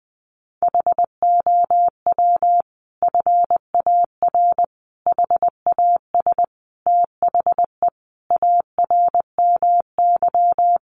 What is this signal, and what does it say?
Morse code, heard as a keyed tone, about 20 wpm, 700 Hz. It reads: HOW FAR HAS THE ARMY
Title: Sentences from Top 1200 Words 20wpm